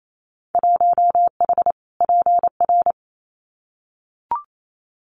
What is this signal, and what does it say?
Morse code, heard as a keyed tone, about 28 wpm, 690 Hz. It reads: 1H PR
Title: Winter Field Day Exchanges Encore 28wpm